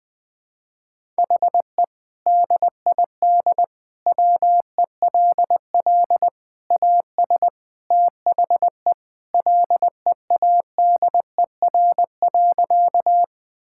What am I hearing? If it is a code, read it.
Morse code, heard as a keyed tone, about 20 wpm, 700 Hz. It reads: HE DID WELL AS THE LEADER.
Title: Sentences with 3-Word Phrases - Spoken Beginning - 20wpm